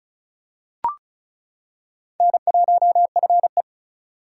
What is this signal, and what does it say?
Morse code, heard as a keyed tone, about 36 wpm, 700 Hz. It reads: N1FE
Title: K1USN SST QSO Practice - 35wpm